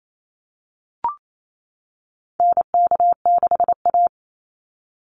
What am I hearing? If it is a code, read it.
Morse code, heard as a keyed tone, about 28 wpm, 700 Hz. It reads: NK6A
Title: NA SOTA Chaser Callsigns 28wpm